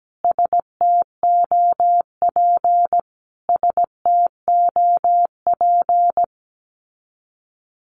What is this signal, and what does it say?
Morse code, heard as a keyed tone, about 17 wpm, 700 Hz. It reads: STOP STOP
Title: Top 500 Words Repeated 17wpm